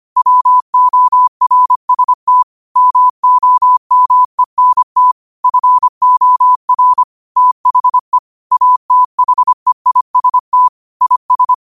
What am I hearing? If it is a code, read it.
Morse code, heard as a keyed tone, about 25 wpm, 1000 Hz. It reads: WORST MOMENT FOR THE ATHEIST IS
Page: Quotes for Fri, 15 Aug 2025 in Morse Code at 25 words per minute.